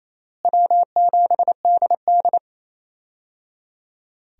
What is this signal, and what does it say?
Morse code, heard as a keyed tone, about 28 wpm, 700 Hz. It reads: W7DD
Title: USA Call Signs - 4 Characters - Repeated 3x - 28wpm